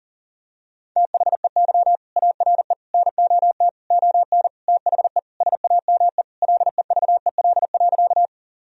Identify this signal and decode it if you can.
Morse code, heard as a keyed tone, about 40 wpm, 700 Hz. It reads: THEY ARE NOT ON THE SAME LEVEL.
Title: Sentences from Top 1000 Words - Review Part 4 - 40wpm